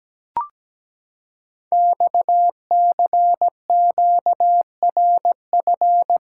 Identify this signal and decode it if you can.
Morse code, heard as a keyed tone, about 17 wpm, 700 Hz. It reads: XCQRF
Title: Code Groups - 5 Letters - 17wpm